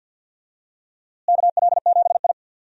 Morse code, heard as a keyed tone, about 51 wpm, 700 Hz. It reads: KL7I
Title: DXCC-100 Call Signs - 4 Characters - Repeated 3x - 50wpm